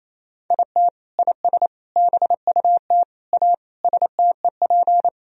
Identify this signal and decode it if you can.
Morse code, heard as a keyed tone, about 28 wpm, 700 Hz. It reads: IT IS BUT A STEP
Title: Sentences from Top 1000 Words - Review Part 2 - 28wpm